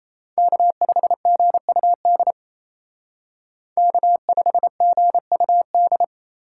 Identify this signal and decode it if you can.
Morse code, heard as a keyed tone, about 35 wpm, 700 Hz. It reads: K5GUD K5GUD